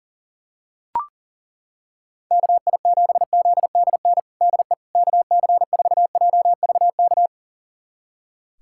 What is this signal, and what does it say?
Morse code, heard as a keyed tone, about 40 wpm, 700 Hz. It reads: KI7ZDN DE KC4JVK
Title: USA Callsign DE Callsign - 6 Characters - 40wpm